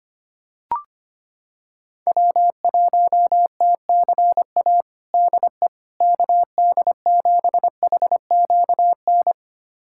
Morse code, heard as a keyed tone, about 25 wpm, 700 Hz. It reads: W1TCA DE KD7HQN
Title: USA Callsign DE Callsign 25wpm